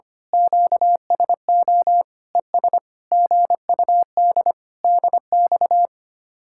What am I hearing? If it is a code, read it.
Morse code, heard as a keyed tone, about 24 wpm, 700 Hz. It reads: QSO ES GUD DX
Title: Speed-Racing Common QSO Phrases - 25wpm